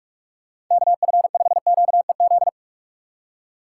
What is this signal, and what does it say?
Morse code, heard as a keyed tone, about 44 wpm, 700 Hz. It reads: KF5QEZ